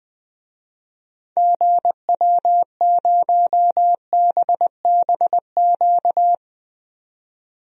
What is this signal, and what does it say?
Morse code, heard as a keyed tone, about 20 wpm, 700 Hz. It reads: GW0BBQ
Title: DXCC-100 Call Signs - 6 Characters - 20wpm